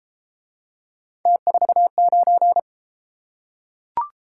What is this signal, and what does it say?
Morse code, heard as a keyed tone, about 31 wpm, 700 Hz. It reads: T49